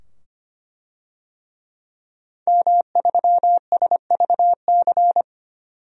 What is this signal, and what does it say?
Morse code, heard as a keyed tone, about 25 wpm, 700 Hz. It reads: M3SVC
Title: DXCC-100 Call Signs - 5 Characters - 25wpm